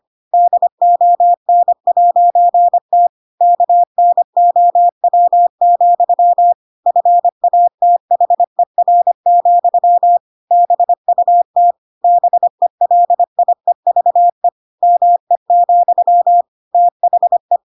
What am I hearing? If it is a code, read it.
Morse code, heard as a keyed tone, about 25 wpm, 700 Hz. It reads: DON'T KNOW, FATHER, BUT BELIEVE ME, THE
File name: Pinocchio / Pinocchio ch7 25 25 ik1hge